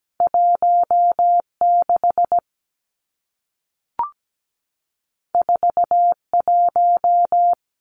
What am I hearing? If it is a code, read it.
Morse code, heard as a keyed tone, about 17 wpm, 700 Hz. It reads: 16 41
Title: Numbers - 2-Digit 17wpm